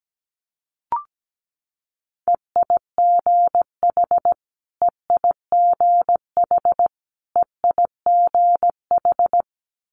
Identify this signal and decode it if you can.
Morse code, heard as a keyed tone, about 17 wpm, 700 Hz. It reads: EIGH EIGH EIGH
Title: Multi-letter Phonemes Repeated 3x 17wpm